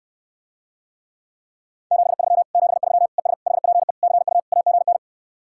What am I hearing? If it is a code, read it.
Morse code, heard as a keyed tone, about 69 wpm, 700 Hz. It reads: 73 73 ES HPE CU AGN